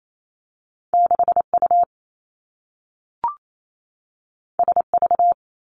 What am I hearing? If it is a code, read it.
Morse code, heard as a keyed tone, about 28 wpm, 700 Hz. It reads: 6 U S V
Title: Two Challenging Letter-Number Combinations 28wpm